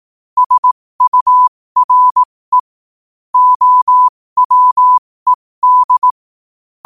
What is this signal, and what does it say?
Morse code, heard as a keyed tone, about 18 wpm, spaced as for 15 wpm, 1000 Hz. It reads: SURE OWED
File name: qotdfiles / 15wpm